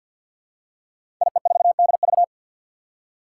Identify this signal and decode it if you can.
Morse code, heard as a keyed tone, about 53 wpm, 700 Hz. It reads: IE4DV
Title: Five Challenging Letter-Number Combinations 50wpm